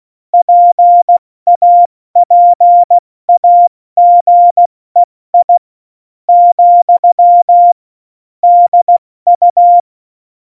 Morse code, heard as a keyed tone, about 16 wpm, spaced as for 13 wpm, 700 Hz. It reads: PAPAGEI, DU
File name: Morsekurs / Witze / Witz 23. Oktober 2018, 70 F80